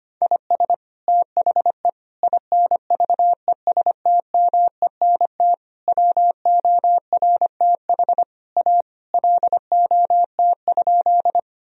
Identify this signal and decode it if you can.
Morse code, heard as a keyed tone, about 25 wpm, 700 Hz. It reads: IS THE INVESTMENT WORTH A LOT?
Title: Sentences from Top 1000 Words - Review Part 4 - 25wpm